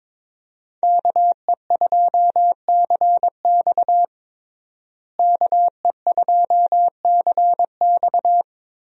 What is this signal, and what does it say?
Morse code, heard as a keyed tone, about 21 wpm, 700 Hz. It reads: KE2CX KE2CX
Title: USA Call Signs - 5 Characters - Repeated 2x - 22wpm